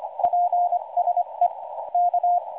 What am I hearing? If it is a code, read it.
Morse code, heard as a keyed tone, about 25 wpm, 710 Hz, with radio noise. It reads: PSE K